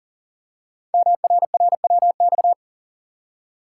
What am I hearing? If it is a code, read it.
Morse code, heard as a keyed tone, about 42 wpm, 700 Hz. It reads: MRRWX
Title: Code Groups - 5 Letters - 40wpm